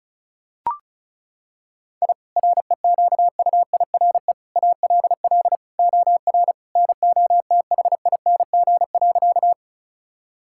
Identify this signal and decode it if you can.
Morse code, heard as a keyed tone, about 36 wpm, 700 Hz. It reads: I REQUIRE ALL OR NOTHING.
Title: Sentences from Top 1000 Words - Review Part 6 - 35wpm